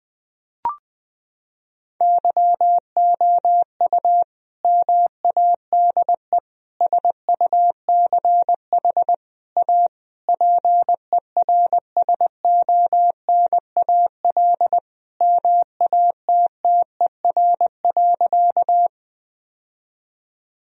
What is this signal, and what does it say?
Morse code, heard as a keyed tone, about 21 wpm, 700 Hz. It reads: YOU MADE SUCH A PERSONAL MATTER.
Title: Sentences from Top 1000 Words - Review Part 4 - 20wpm